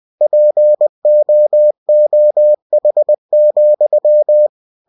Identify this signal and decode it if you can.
Morse code, heard as a keyed tone, about 20 wpm, 600 Hz. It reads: POOH,
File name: cw-books / winnie-the-pooh / winnie-20wpm-page-0018